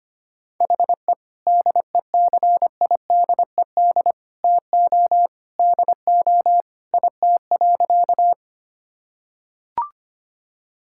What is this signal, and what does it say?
Morse code, heard as a keyed tone, about 25 wpm, 700 Hz. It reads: HE DECIDED TO DO IT.
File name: practice / 25wpm / Sentences with 3-Word Phrases - 25wpm